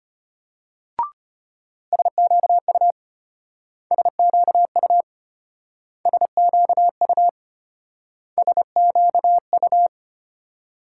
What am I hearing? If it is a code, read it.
Morse code, heard as a keyed tone, about 33 wpm, 700 Hz. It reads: SQU SQU SQU SQU